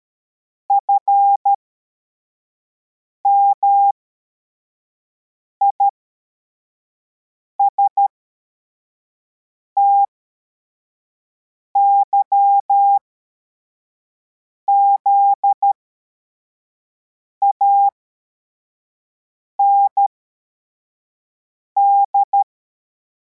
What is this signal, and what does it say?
Morse code, heard as a keyed tone, about 13 wpm, spaced as for 4 wpm, 800 Hz. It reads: FMISTYZAND
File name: formation / cw / Text63